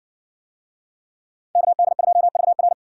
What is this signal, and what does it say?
Morse code, heard as a keyed tone, about 57 wpm, 700 Hz. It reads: KD2VR